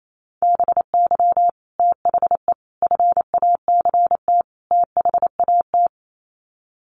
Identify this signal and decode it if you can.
Morse code, heard as a keyed tone, about 28 wpm, 700 Hz. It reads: BY THE FACT THAT
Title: Sets of 4 Words 28wpm